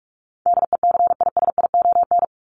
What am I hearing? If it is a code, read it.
Morse code, heard as a keyed tone, about 44 wpm, 700 Hz. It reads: DECISION
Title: Top 751-1000 Words 45wpm